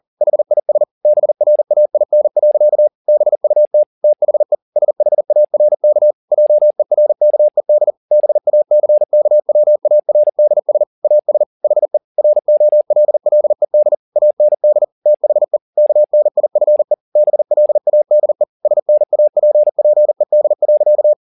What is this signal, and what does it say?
Morse code, heard as a keyed tone, about 40 wpm, 600 Hz. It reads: HIS BRAIN. BUT THE SHARK JERKED BACKWARDS AS HE ROLLED AND THE KNIFE BLADE SNAPPED.
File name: the old man and the sea 2 40wpm